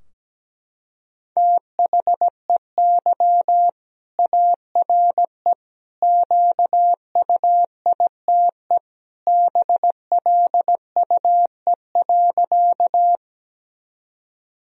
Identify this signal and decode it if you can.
Morse code, heard as a keyed tone, about 17 wpm, 700 Hz. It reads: THEY ARE QUITE BLUE.
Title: Sentences from Top 1000 Words - Review Part 3 - 17wpm